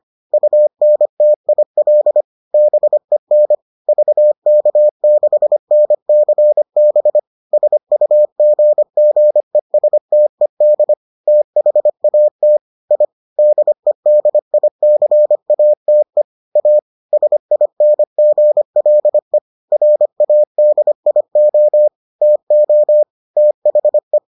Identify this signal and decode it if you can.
Morse code, heard as a keyed tone, about 25 wpm, 600 Hz. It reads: UNTIL BEN VK6NCB SUGGESTED THAT I DEDICATE A SINGLE RADIO TO THE